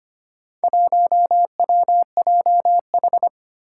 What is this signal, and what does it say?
Morse code, heard as a keyed tone, about 25 wpm, 700 Hz. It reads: 1WJH
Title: Four Challenging Letter-Number Combinations 25wpm